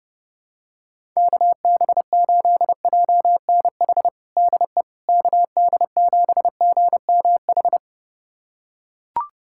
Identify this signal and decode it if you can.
Morse code, heard as a keyed tone, about 28 wpm, 700 Hz. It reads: KB8JNH DE KD7GMH
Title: USA Callsign DE Callsign - 6 Characters - 30wpm